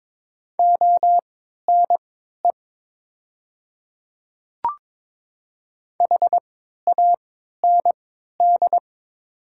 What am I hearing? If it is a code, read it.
Morse code, heard as a keyed tone, about 23 wpm, spaced as for 13 wpm, 690 Hz. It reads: ONE HAND
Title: 3x Character Spacing - Top 500 Words - 22wpm